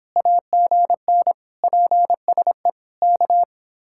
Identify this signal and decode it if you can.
Morse code, heard as a keyed tone, about 26 wpm, 700 Hz. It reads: AGN PSE K